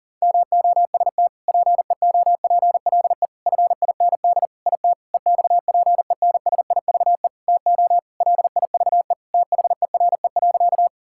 Morse code, heard as a keyed tone, about 40 wpm, 690 Hz. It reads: MOST PEOPLE FIND IT EXPENSIVE TO LIVE THERE.
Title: Sentences from Top 1000 Words - Review Part 6 - 40wpm